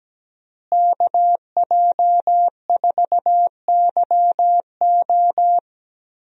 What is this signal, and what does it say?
Morse code, heard as a keyed tone, about 17 wpm, 700 Hz. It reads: KJ4YO
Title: USA Call Signs - 5 Characters - Repeated 2x - 17wpm